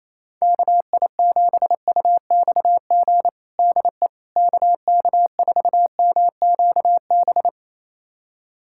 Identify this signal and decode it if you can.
Morse code, heard as a keyed tone, about 28 wpm, 700 Hz. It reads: KI7UXG DE KK4MQB
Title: USA Callsign DE Callsign - 6 Characters - 28wpm